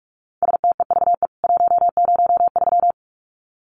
Morse code, heard as a keyed tone, about 44 wpm, 700 Hz. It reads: STEVE 103
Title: CWT Exchange 45wpm